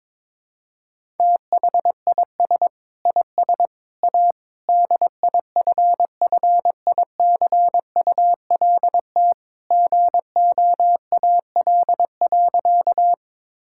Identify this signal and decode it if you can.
Morse code, heard as a keyed tone, about 22 wpm, 700 Hz. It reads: THIS IS A DIFFICULT GOAL.
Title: Sentences from Top 1000 Words - Review Part 5 - 22wpm